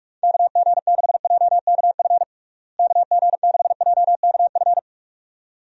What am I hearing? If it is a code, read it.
Morse code, heard as a keyed tone, about 41 wpm, 700 Hz. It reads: KG6JKF KG6JKF
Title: USA Call Signs - 6 Characters - Repeated 2x - 45wpm